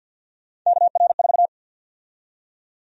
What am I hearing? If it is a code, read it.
Morse code, heard as a keyed tone, about 46 wpm, 700 Hz. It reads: KR4
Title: USA Call Sign Prefixes - 50wpm